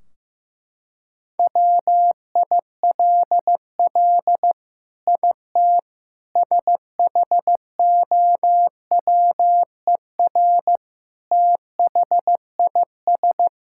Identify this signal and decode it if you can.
Morse code, heard as a keyed tone, about 15 wpm, 700 Hz. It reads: WILL IT SHOWER THIS
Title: Sentences from Top 1600 Words 15wpm